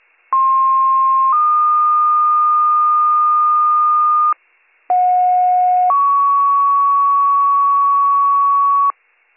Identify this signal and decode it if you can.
Two-tone page, heard as a one-second tone, then a three-second tone, about 1060 then 1220 Hz, about 730 then 1120 Hz. It is heard from 0.3 to 8.9 s.